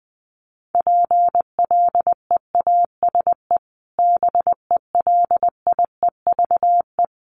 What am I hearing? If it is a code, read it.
Morse code, heard as a keyed tone, about 20 wpm, 700 Hz. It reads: PLEASE BELIEVE
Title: Sentences from Top 1500 Words 20wpm